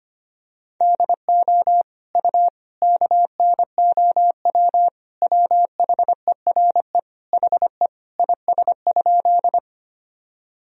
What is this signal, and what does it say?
Morse code, heard as a keyed tone, about 25 wpm, 700 Hz. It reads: DO U KNOW WHERE HE IS?
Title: Sentences from Top 300 Words and Abbreviations 25wpm